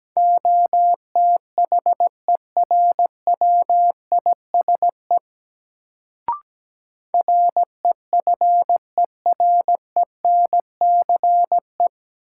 Morse code, heard as a keyed tone, about 17 wpm, 700 Hz. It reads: OTHERWISE REFERENCE
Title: Sentences from Top 1200 Words - New Word Drill Alt - 17wpm